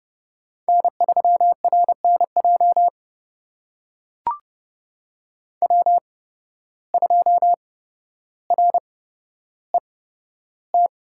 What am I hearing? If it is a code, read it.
Morse code, heard as a keyed tone, about 32 wpm, spaced as for 13 wpm, 700 Hz. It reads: N3RNJ W2RET
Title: USA Call Signs - 5 Characters - 8x Character Spacing - 30wpm